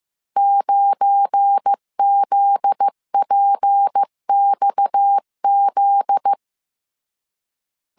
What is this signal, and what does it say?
Morse code, heard as a keyed tone, about 15 wpm, 800 Hz. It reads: 9ZPXZ
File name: CW / DET / 1 9 1500C S C 14wpm 3245